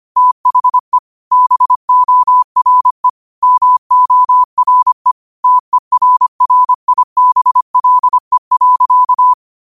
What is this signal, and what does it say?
Morse code, heard as a keyed tone, about 25 wpm, 1000 Hz. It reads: THE BORE MORE TERRIBLE.
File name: qotdfiles / 25wpm